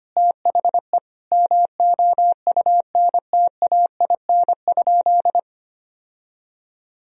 Morse code, heard as a keyed tone, about 25 wpm, 700 Hz. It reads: THE MOUNTAIN?
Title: Sentences from Top 1000 Words - Review Part 1 - 25wpm